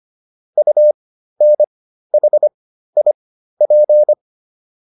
Morse code, heard as a keyed tone, about 25 wpm, spaced as for 13 wpm, 600 Hz. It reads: UNHIP